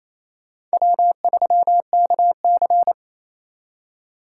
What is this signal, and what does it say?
Morse code, heard as a keyed tone, about 32 wpm, 700 Hz. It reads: W3KC
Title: Speed Racing - USA Call Signs 28wpm